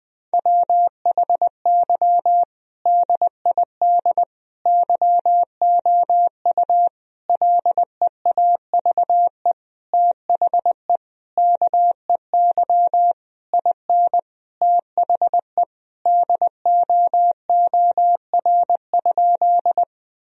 Morse code, heard as a keyed tone, about 20 wpm, 700 Hz. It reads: WHY DID YOU LEAVE THE KEY IN THE DOOR?
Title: Sentences from Top 1000 Words - Review Part 4 - 20wpm